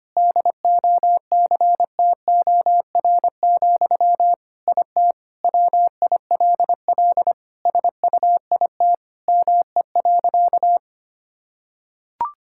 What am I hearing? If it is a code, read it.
Morse code, heard as a keyed tone, about 25 wpm, 700 Hz. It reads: DOCTOR, IT WILL SUIT ME.
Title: Sentences from Top 1600 Words 25wpm